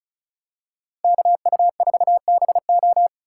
Morse code, heard as a keyed tone, about 36 wpm, 700 Hz. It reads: KU4BO
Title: USA Call Signs - 5 Characters - 4x Character Spacing - 35wpm